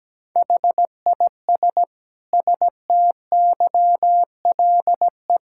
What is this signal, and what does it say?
Morse code, heard as a keyed tone, about 17 wpm, 700 Hz. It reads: HIS STYLE
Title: Sentences from Top 1000 Words - Review Part 5 - 17wpm